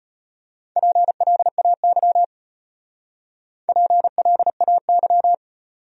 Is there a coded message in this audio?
Yes: Morse code from 0.8 to 5.3 s, heard as a keyed tone, about 32 wpm, 700 Hz. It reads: PLAY PLAY